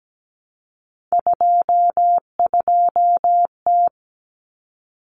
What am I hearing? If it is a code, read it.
Morse code, heard as a keyed tone, about 17 wpm, 690 Hz. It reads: 22T
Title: Common Cut Numbers - 3-Digit 17wpm